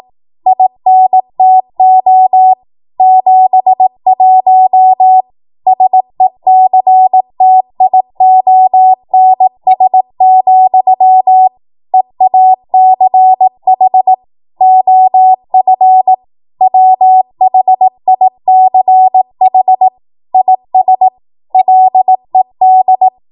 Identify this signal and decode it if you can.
Morse code, heard as a keyed tone, about 18 wpm, 750 Hz. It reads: INTO 71 SECTIONS, EACH OF WHICH IS LED
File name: Morse / 18WPM